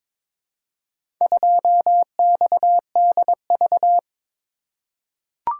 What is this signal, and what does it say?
Morse code, heard as a keyed tone, about 21 wpm, 700 Hz. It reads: 2XDV
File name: practice / 22wpm / Four Letter-Number Combinations - 22wpm